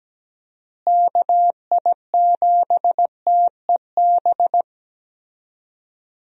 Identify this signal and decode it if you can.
Morse code, heard as a keyed tone, about 17 wpm, 700 Hz. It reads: KI7TEB
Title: USA Call Signs - 6 Characters - Repeated 3x - 17wpm